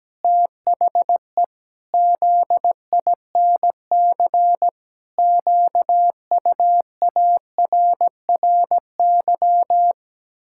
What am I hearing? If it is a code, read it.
Morse code, heard as a keyed tone, about 17 wpm, 700 Hz. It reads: THE ZINC QUARRY
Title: Pangram Challenge v2 - 17wpm